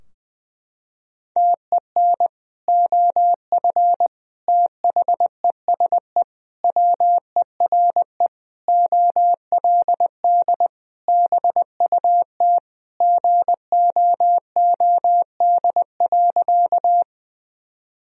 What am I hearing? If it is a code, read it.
Morse code, heard as a keyed tone, about 21 wpm, 700 Hz. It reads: TEN OF THESE WERE OLD BUT GOOD.
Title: Sentences from Top 1000 Words - Review Part 2 - 20wpm